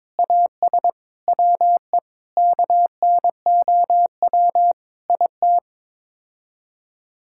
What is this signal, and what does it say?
Morse code, heard as a keyed tone, about 22 wpm, 700 Hz. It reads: AS WE KNOW IT
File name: Sets of 9 Words - 22wpm